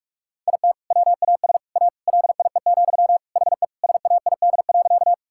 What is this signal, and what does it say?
Morse code, heard as a keyed tone, about 46 wpm, 700 Hz. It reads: IT WAS A LIE, HE SAID.
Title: Sentences from Top 1600 Words 45wpm